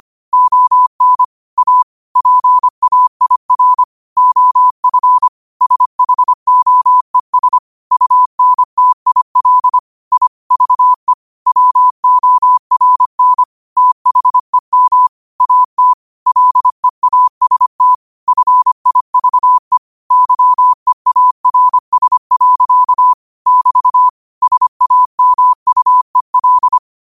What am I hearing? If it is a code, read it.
Morse code, heard as a keyed tone, about 25 wpm, 1000 Hz. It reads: ON A PAIR OF SHOES UNTIL I VE WORN THEM AT LEAST FIVE YEARS. <BT> SAMUEL
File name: qotdfiles / 25wpm